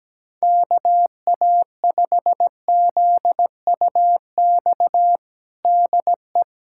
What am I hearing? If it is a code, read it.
Morse code, heard as a keyed tone, about 17 wpm, 700 Hz. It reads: KA5ZUX DE
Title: USA Callsign DE Callsign 17wpm